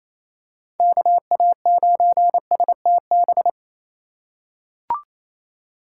Morse code, heard as a keyed tone, about 28 wpm, 700 Hz. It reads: KA9STB
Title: USA Call Signs - 6 Characters - 2x Character Spacing - 28wpm